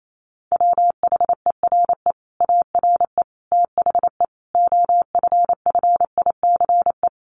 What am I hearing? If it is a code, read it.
Morse code, heard as a keyed tone, about 28 wpm, 700 Hz. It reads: WHERE ARE THE OFFICE
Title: Sentences from Top 1200 Words 28wpm